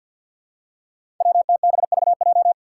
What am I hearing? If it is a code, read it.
Morse code, heard as a keyed tone, about 51 wpm, 700 Hz. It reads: WTBVJ
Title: Five Challenging Letter-Number Combinations 50wpm